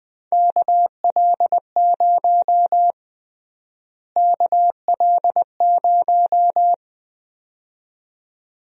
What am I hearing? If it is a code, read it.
Morse code, heard as a keyed tone, about 21 wpm, 700 Hz. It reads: KL0 KL0
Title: USA Call Sign Prefixes - Repeated 2x - 20wpm